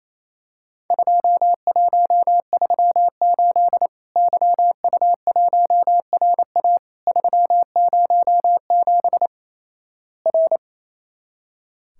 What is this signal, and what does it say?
Morse code, heard as a keyed tone, about 28 wpm, 700 Hz. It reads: 2138 YU1RA 307 R
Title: Worked All Europe DX Contest - QTC Exchange - 28wpm